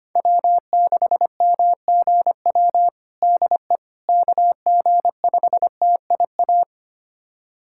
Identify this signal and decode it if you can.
Morse code, heard as a keyed tone, about 25 wpm, 700 Hz. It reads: W6MGW DE KG5TIA
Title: USA Callsign DE Callsign 25wpm